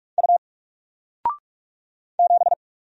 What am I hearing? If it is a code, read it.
Morse code, heard as a keyed tone, about 44 wpm, 700 Hz. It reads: U 7
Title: Single Character (Letter:Number:Punctuation:Prosign) 45wpm